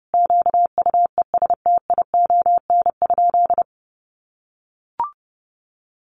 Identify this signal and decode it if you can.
Morse code, heard as a keyed tone, about 30 wpm, 700 Hz. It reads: QUESTION?
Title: Sentences from Top 1100 Words 30wpm